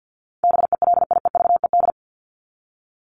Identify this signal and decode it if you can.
Morse code, heard as a keyed tone, about 50 wpm, 700 Hz. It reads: BELIEVED
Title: ED Suffix - Top 100 Words with Base Words 50wpm